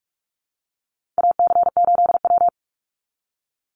Morse code, heard as a keyed tone, about 44 wpm, 700 Hz. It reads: AC8W
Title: Call Signs - 45wpm